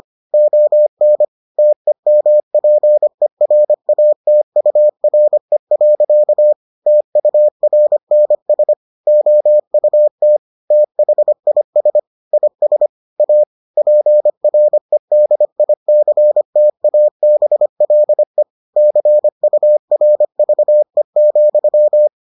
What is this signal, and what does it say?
Morse code, heard as a keyed tone, about 25 wpm, 600 Hz. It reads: ON TEMPERATURE. TURNS OUT THIS IS A PREDICTABLE CURVE,